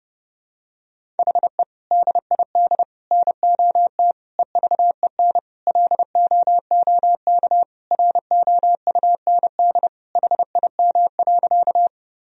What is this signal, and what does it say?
Morse code, heard as a keyed tone, about 32 wpm, 700 Hz. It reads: HE DID NOT EVEN LOOK ROUND HIM.
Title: Sentences from Top 1000 Words - Review Part 1 - 30wpm